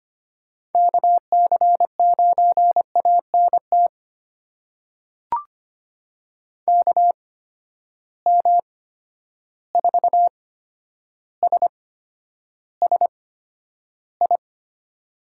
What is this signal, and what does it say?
Morse code, heard as a keyed tone, about 25 wpm, spaced as for 11 wpm, 700 Hz. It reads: KC9ANT KM4SSI
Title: USA Call Signs - 6 Characters - 8x Character Spacing - 25wpm